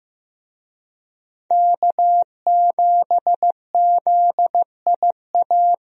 Morse code, heard as a keyed tone, about 15 wpm, 700 Hz. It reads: K7ZIA
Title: USA Call Signs - 5 Characters - Repeated 2x - 15wpm